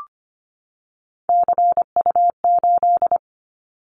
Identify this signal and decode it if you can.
Morse code, heard as a keyed tone, about 25 wpm, 700 Hz. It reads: CU8
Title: Callsign Prefixes for Top 100 DXCC - 25wpm